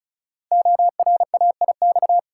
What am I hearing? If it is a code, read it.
Morse code, heard as a keyed tone, about 36 wpm, 700 Hz. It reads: ORAIX
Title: Code Groups - 5 Letters - 35wpm